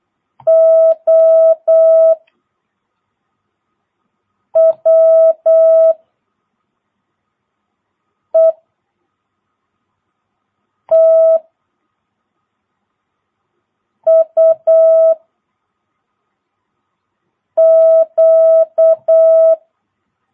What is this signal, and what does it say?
Morse code, heard as a keyed tone, about 8 wpm, 1300 Hz. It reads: O W E T U Q
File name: jh8cbh / 2022cw